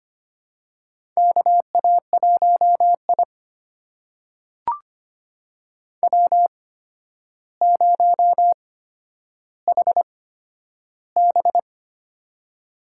Morse code, heard as a keyed tone, about 25 wpm, spaced as for 11 wpm, 700 Hz. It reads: KA1I W0HB
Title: USA Call Signs - 8x Character Spacing - 25wpm